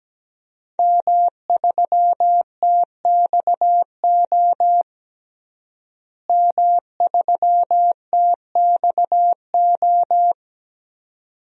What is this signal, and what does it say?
Morse code, heard as a keyed tone, about 17 wpm, 700 Hz. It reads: M3TXO M3TXO
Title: DXCC-100 Call Signs - 5 Characters - Repeated 2x - 17wpm